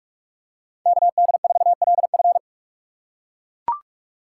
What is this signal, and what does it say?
Morse code, heard as a keyed tone, about 41 wpm, 700 Hz. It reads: KD4LF